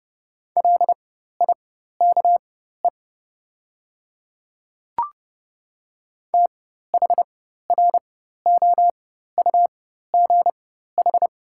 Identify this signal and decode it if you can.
Morse code, heard as a keyed tone, about 30 wpm, spaced as for 14 wpm, 700 Hz. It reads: LIKE THROUGH
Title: 4x Character Spacing - Top 200 Words - 30wpm